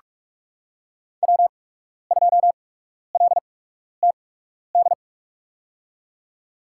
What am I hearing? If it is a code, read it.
Morse code, heard as a keyed tone, about 42 wpm, spaced as for 12 wpm, 700 Hz. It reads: W2LTD
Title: USA Call Signs - 5 Characters - 8x Character Spacing - 45wpm